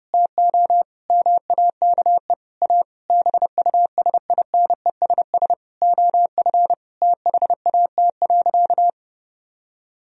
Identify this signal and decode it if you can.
Morse code, heard as a keyed tone, about 32 wpm, 700 Hz. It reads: TO MAKE A BUSINESS OF THAT.
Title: Sentences from Top 1000 Words - Review Part 4 - 30wpm